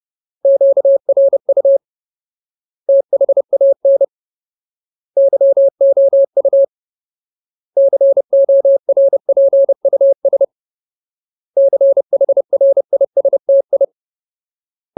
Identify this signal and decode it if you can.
Morse code, heard as a keyed tone, about 30 wpm, 550 Hz. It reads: QRU THAN YOU CORPUS CHRISTI
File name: Ultimate Word List / 30WPM Ultimate